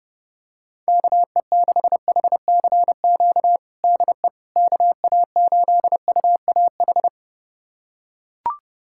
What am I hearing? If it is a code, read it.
Morse code, heard as a keyed tone, about 32 wpm, 700 Hz. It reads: KE6HCQ DE KA8UAH
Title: USA Callsign DE Callsign 30wpm